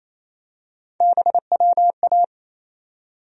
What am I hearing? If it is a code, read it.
Morse code, heard as a keyed tone, about 28 wpm, 700 Hz. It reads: BWA
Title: USA Call Sign Postfixes - 3 Letters 28wpm